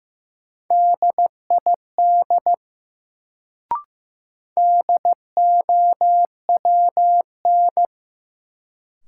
Morse code, heard as a keyed tone, about 15 wpm, 700 Hz. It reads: DID DOWN
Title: Sets of 2 Words - New Word Drill - 15wpm